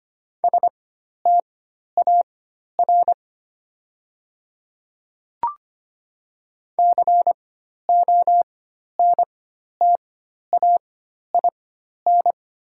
Morse code, heard as a keyed tone, about 25 wpm, spaced as for 12 wpm, 700 Hz. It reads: STAR CONTAIN
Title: 4x Character Spacing - Top 500 Words - 25wpm